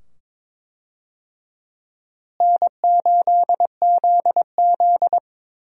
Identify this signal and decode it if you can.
Morse code, heard as a keyed tone, about 22 wpm, 700 Hz. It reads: N8ZZ
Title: USA Call Signs - 4 Characters - Repeated 2x - 22wpm